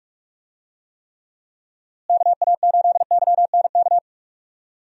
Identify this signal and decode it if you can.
Morse code, heard as a keyed tone, about 47 wpm, 700 Hz. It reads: KA8YNK
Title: USA Call Signs - 6 Characters - Repeated 3x - 45wpm